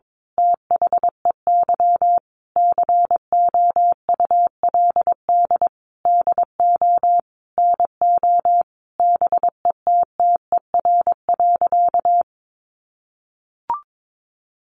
Morse code, heard as a keyed tone, about 22 wpm, 700 Hz. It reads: THEY COULD DO NO BETTER.
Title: Sentences from Top 1300 Words 22wpm